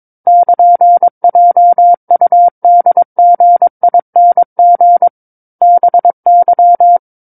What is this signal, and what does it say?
Morse code, heard as a keyed tone, about 22 wpm, 700 Hz. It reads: <KN>JUDGING BY
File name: 06-Chapter-06 22 WPM